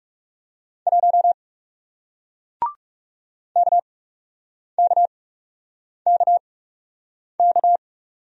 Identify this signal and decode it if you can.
Morse code, heard as a keyed tone, about 39 wpm, spaced as for 8 wpm, 700 Hz. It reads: 1 KKKK